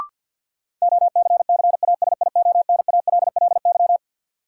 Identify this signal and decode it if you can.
Morse code, heard as a keyed tone, about 51 wpm, 700 Hz. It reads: OCCASIONALLY
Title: Sentences from Top 1300 Words - New Word Drill - 50wpm